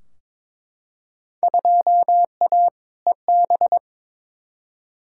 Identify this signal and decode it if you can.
Morse code, heard as a keyed tone, about 23 wpm, 700 Hz. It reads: 2A EB
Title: Field Day Exchanges - 22wpm